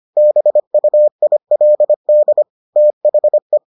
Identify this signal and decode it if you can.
Morse code, heard as a keyed tone, about 25 wpm, 600 Hz. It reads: BUILD THE